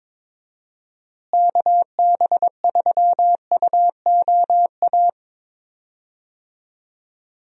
Morse code, heard as a keyed tone, about 22 wpm, 700 Hz. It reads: KB3UOA
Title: USA Call Signs - 6 Characters - Repeated 2x - 22wpm